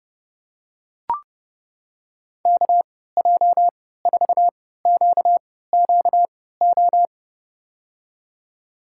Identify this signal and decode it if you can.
Morse code, heard as a keyed tone, about 30 wpm, spaced as for 17 wpm, 700 Hz. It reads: KJ4QQO
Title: USA Call Signs - 3x Character Spacing - 30wpm